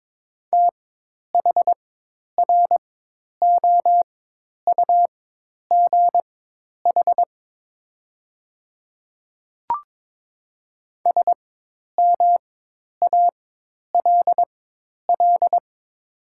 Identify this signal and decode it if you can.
Morse code, heard as a keyed tone, about 22 wpm, spaced as for 10 wpm, 700 Hz. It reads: THROUGH SMALL
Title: 4x Character Spacing - Top 200 Words - 22wpm